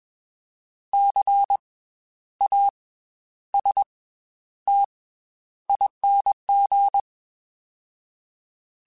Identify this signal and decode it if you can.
Morse code, heard as a keyed tone, about 21 wpm, 800 Hz. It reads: C A S T ING